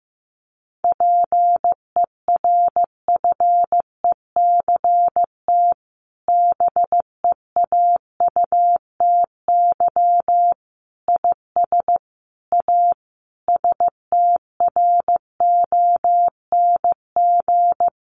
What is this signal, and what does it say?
Morse code, heard as a keyed tone, about 15 wpm, 700 Hz. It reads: PERFECT BEAUTY IS A STRONG
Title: Sentences from Top 1200 Words 15wpm